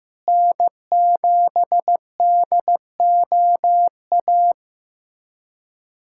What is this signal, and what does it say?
Morse code, heard as a keyed tone, about 15 wpm, 700 Hz. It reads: N7DOA
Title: USA Callsign DE Callsign - 5 Characters - 15wpm